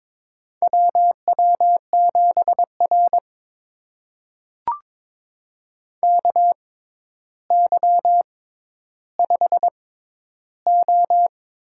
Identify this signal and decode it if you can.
Morse code, heard as a keyed tone, about 22 wpm, spaced as for 11 wpm, 700 Hz. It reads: WW7R KY5O
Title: USA Call Signs - 4 Characters - 6x Character Spacing - 22wpm